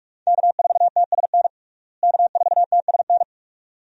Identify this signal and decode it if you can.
Morse code, heard as a keyed tone, about 41 wpm, 700 Hz. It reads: K4TSN K4TSN
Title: USA Call Signs - 5 Characters - Repeated 2x - 45wpm